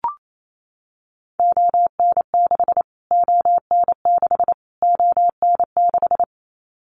Morse code, heard as a keyed tone, about 28 wpm, 700 Hz. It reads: ON6 ON6 ON6
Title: Callsign Prefixes for Top 25 DXCC - Repeated 3x - 28wpm